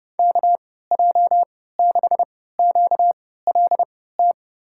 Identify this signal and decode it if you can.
Morse code, heard as a keyed tone, about 30 wpm, spaced as for 17 wpm, 700 Hz. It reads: KJ6QLT
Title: USA Call Signs - 6 Characters - 3x Character Spacing - 30wpm